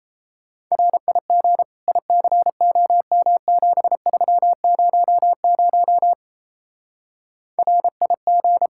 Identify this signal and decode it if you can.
Morse code, heard as a keyed tone, about 35 wpm, 700 Hz. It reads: RIG ICOM7300 RIG
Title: Speed-Racing Common QSO Phrases Encore - 28wpm